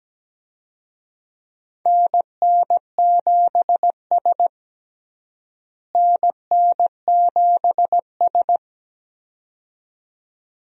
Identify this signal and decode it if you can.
Morse code, heard as a keyed tone, about 17 wpm, 700 Hz. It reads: NN7S NN7S
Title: USA Call Signs - 4 Characters - Repeated 2x - 17wpm